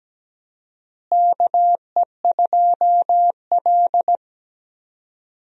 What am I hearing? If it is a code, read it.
Morse code, heard as a keyed tone, about 17 wpm, 700 Hz. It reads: KE2L
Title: USA Call Signs - 4 Characters - Repeated 3x - 17wpm